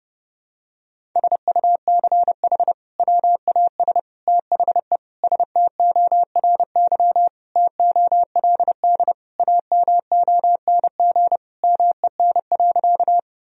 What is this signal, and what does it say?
Morse code, heard as a keyed tone, about 30 wpm, 700 Hz. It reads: SUCH WAS THE STORY TOLD AMONG MEN.
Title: Sentences from Top 1000 Words - Review Part 3 - 30wpm